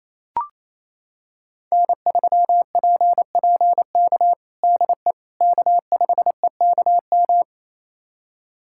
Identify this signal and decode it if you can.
Morse code, heard as a keyed tone, about 28 wpm, 700 Hz. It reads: N3PPK DE K5EKM
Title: USA Callsign DE Callsign - 5 Characters - 28wpm